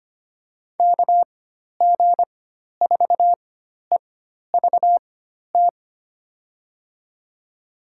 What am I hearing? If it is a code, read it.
Morse code, heard as a keyed tone, about 25 wpm, spaced as for 12 wpm, 700 Hz. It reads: KG4EVT
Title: USA Call Signs - 6 Characters - 4x Character Spacing - 25wpm